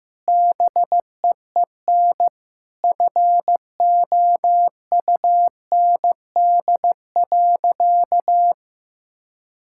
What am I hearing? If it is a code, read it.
Morse code, heard as a keyed tone, about 15 wpm, 700 Hz. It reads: BEEN FOUND.
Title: Sentences from Top 1000 Words - Review Part 5 - 15wpm